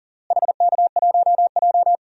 Morse code, heard as a keyed tone, about 40 wpm, 700 Hz. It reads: HK1J
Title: DXCC-100 Call Signs - 4 Characters - Repeated 2x - 40wpm